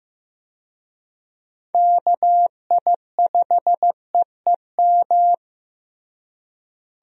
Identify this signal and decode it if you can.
Morse code, heard as a keyed tone, about 15 wpm, 700 Hz. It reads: KI5EEM
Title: USA Call Signs - 6 Characters - Repeated 2x - 15wpm